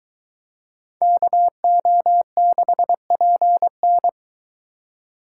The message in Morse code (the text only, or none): KO6PN